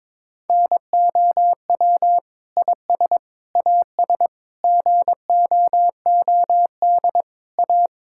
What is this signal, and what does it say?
Morse code, heard as a keyed tone, about 22 wpm, 700 Hz. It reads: NOW IS AS GOOD A
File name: Sets of 9 Words - 22wpm